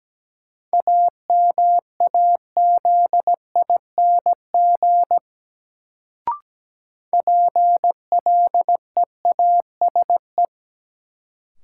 Morse code, heard as a keyed tone, about 17 wpm, 700 Hz. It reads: AMAZING PLEASE
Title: Sentences from Top 1500 Words - New Word Drill Alt - 17wpm